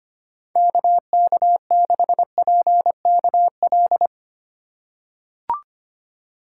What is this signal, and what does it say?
Morse code, heard as a keyed tone, about 25 wpm, 700 Hz. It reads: KK6PKL
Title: USA Call Signs - 6 Characters - 2x Character Spacing - 25wpm